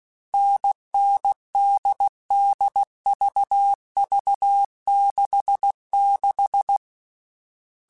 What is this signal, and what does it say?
Morse code, heard as a keyed tone, about 16 wpm, 800 Hz. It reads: NN DD VV 66
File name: Short Runs / Short Runs-mp3s / NDB NDB6 @15